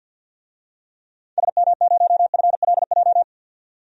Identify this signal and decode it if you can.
Morse code, heard as a keyed tone, about 52 wpm, 700 Hz. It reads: SM0FLJ
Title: DXCC-100 Call Signs - 6 Characters - 50wpm